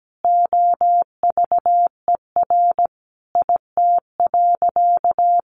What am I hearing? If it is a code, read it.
Morse code, heard as a keyed tone, about 17 wpm, 700 Hz. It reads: OVER IT.
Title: Sentences from Top 100 Words 17wpm